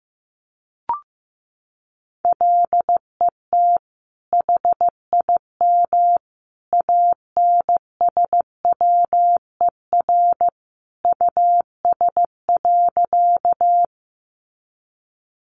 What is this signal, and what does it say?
Morse code, heard as a keyed tone, about 15 wpm, 700 Hz. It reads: LET HIM ANSWER US.
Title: Sentences from Top 300 Words 15wpm